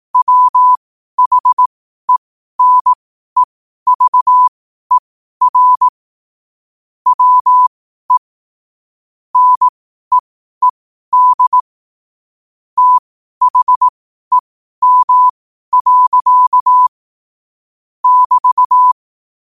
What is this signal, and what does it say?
Morse code, heard as a keyed tone, about 18 wpm, spaced as for 12 wpm, 1000 Hz. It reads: WHENEVER WE NEED THEM. <BT>
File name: qotdfiles / 12wpm